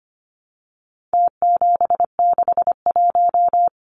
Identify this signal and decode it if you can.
Morse code, heard as a keyed tone, about 25 wpm, 700 Hz. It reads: T761
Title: Numbers - Contest Style - 4-Digit 25wpm